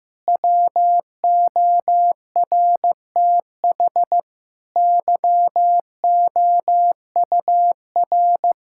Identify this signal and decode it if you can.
Morse code, heard as a keyed tone, about 15 wpm, 700 Hz. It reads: WORTH YOUR
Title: Sentences from Top 1600 Words 15wpm